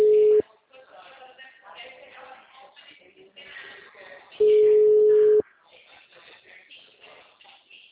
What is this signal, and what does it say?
Ringback tone (Europe, 425 Hz).